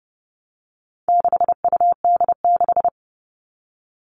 Morse code, heard as a keyed tone, about 30 wpm, 700 Hz. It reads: 6UD6
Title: All Cut Numbers - 4-Digit 30wpm